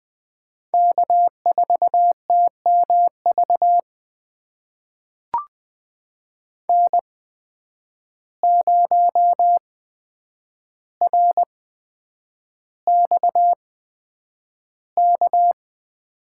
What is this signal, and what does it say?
Morse code, heard as a keyed tone, about 20 wpm, spaced as for 9 wpm, 700 Hz. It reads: K4TMV N0RXK
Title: USA Call Signs - 5 Characters - 8x Character Spacing - 20wpm